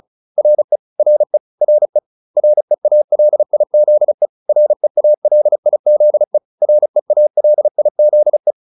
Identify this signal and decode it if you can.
Morse code, heard as a keyed tone, about 34 wpm, 600 Hz. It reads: RE RE RE REALIZE REALIZE REALIZE